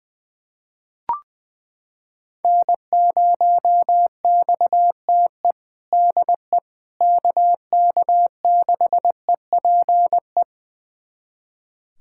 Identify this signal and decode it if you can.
Morse code, heard as a keyed tone, about 20 wpm, 700 Hz. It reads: N0XTE DE KK6EPE
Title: USA Callsign DE Callsign 20wpm